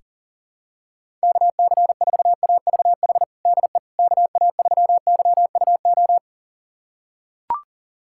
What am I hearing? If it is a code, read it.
Morse code, heard as a keyed tone, about 40 wpm, 700 Hz. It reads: KC4AVH DE KA3YUO
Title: USA Callsign DE Callsign - 6 Characters - 40wpm